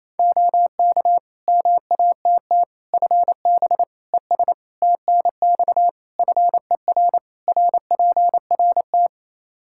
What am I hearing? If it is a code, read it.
Morse code, heard as a keyed tone, about 28 wpm, 700 Hz. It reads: OK MATT FB ES TNX FER RPRT
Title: Practice QSO - Exchange 2 - Info - 28wpm